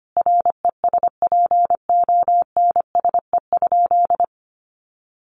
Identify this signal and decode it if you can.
Morse code, heard as a keyed tone, about 25 wpm, 700 Hz. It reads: RESPONSE?
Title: Sentences from Top 1400 Words 25wpm